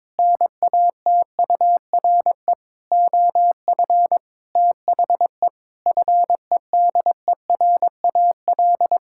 Morse code, heard as a keyed tone, about 22 wpm, 700 Hz. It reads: NATURE OF THE FEDERAL
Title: Sentences from Top 1000 Words - Review Part 5 - 22wpm